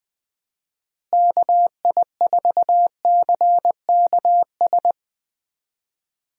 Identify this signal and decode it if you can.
Morse code, heard as a keyed tone, about 21 wpm, 700 Hz. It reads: KI4CKS
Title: USA Call Signs - 6 Characters - Repeated 3x - 20wpm